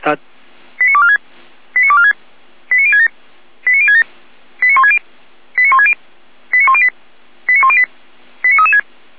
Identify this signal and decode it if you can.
Five-tone selective call (ZVEI), heard as quick runs of five tones, tones of about 70 ms, with speech. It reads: 89246 89246 89067 89067 89160 89160 89108 89108 89336